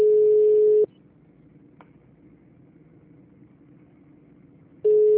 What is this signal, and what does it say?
Ringback tone (Europe, 425 Hz).